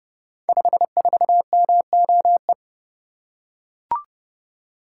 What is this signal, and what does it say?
Morse code, heard as a keyed tone, about 30 wpm, 700 Hz. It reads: 54MOE
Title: Code Groups - 5 Letter:Numbers - 30wpm